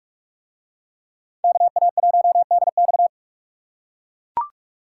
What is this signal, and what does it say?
Morse code, heard as a keyed tone, about 39 wpm, 700 Hz. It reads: KA1DX